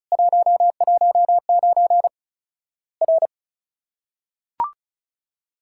Morse code, heard as a keyed tone, about 36 wpm, 700 Hz. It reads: 119 R
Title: Worked All Europe DX Contest - QTC Exchange - 35wpm